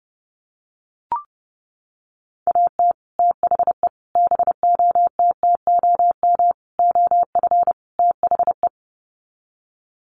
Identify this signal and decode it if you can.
Morse code, heard as a keyed tone, about 30 wpm, 700 Hz. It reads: AT THE BOTTOM OF THE
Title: Sets of 5 Words 30wpm